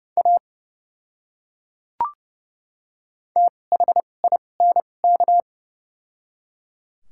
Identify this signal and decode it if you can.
Morse code, heard as a keyed tone, about 30 wpm, spaced as for 22 wpm, 700 Hz. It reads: A THINK
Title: 2x Character Spacing - Top 200 Words - 30wpm